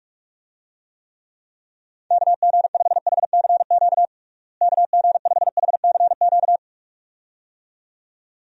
Morse code, heard as a keyed tone, about 46 wpm, 700 Hz. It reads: KG5HCQ KG5HCQ
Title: USA Call Signs - 6 Characters - Repeated 2x - 45wpm